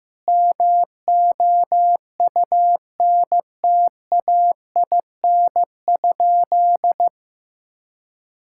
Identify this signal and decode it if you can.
Morse code, heard as a keyed tone, about 16 wpm, 700 Hz. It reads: MOUNTAIN?
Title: Sentences from Top 1000 Words - Review Part 5 - 15wpm